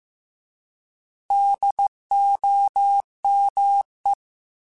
Morse code, heard as a keyed tone, about 15 wpm, 800 Hz. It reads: DOME
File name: Alphabetical / Alphabetical mp3s / Ds @13